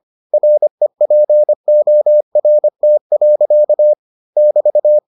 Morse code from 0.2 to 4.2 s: REPORT.

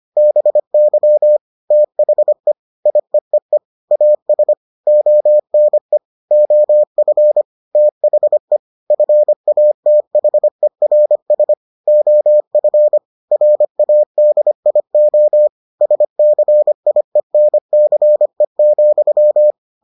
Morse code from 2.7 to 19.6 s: IEEE AS ONE OF THE FATHERS OF RADIO SCIENCE,

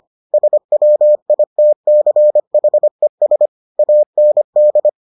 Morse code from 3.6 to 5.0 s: AND

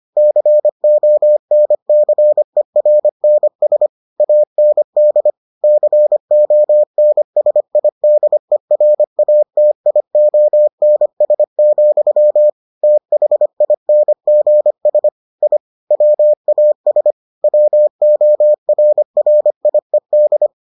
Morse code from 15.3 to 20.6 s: I WAS WORRIED

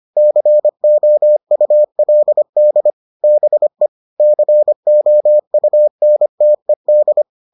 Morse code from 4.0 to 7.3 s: COUNTED

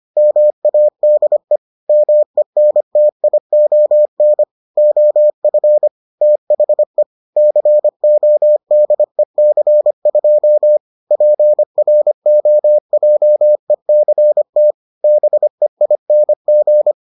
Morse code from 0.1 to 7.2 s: MADE MENTION OF THE

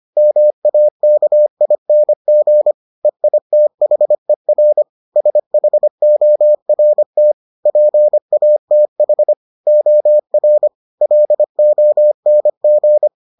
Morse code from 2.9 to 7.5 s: EITHER SHORT

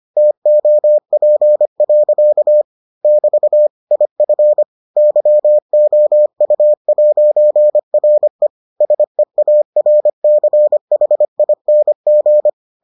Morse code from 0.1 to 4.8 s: TOP. <BT> IF

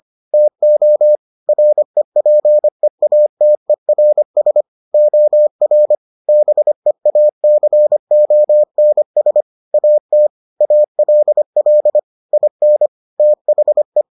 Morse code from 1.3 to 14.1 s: REPEATERS OR BEACONS AT ALL IN THE

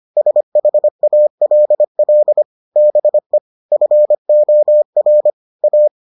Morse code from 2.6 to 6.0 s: BE FOR A

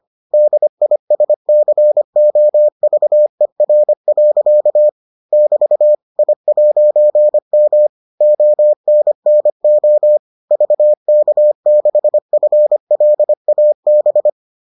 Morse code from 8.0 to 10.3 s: ONNO

